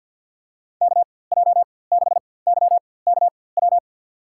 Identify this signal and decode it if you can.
Morse code, heard as a keyed tone, about 52 wpm, spaced as for 24 wpm, 700 Hz. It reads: KJ6YKW
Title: USA Call Signs - 6 Characters - 4x Character Spacing - 50wpm